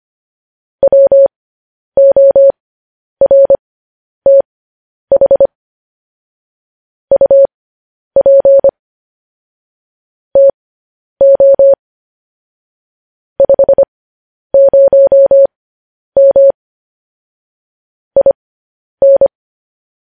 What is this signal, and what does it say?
Morse code, heard as a keyed tone, about 25 wpm, spaced as for 10 wpm, 550 Hz. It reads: WORTH UP TO 50M IN
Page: news headlines in morse code at 10 wpm